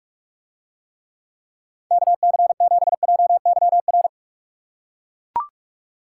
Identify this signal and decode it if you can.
Morse code, heard as a keyed tone, about 44 wpm, 700 Hz. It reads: KC7JYR